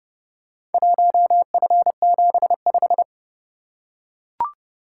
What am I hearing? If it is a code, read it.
Morse code, heard as a keyed tone, about 27 wpm, 700 Hz. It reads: 1F75